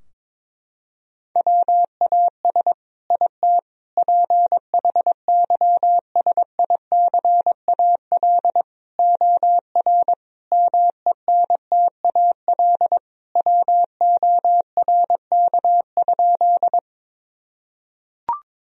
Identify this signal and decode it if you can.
Morse code, heard as a keyed tone, about 21 wpm, 700 Hz. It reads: WAS IT PHYSICAL OR MENTAL WORK?
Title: Sentences from Top 1000 Words - Review Part 4 - 22wpm